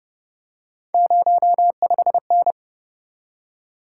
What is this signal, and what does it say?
Morse code, heard as a keyed tone, about 31 wpm, 700 Hz. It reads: 05N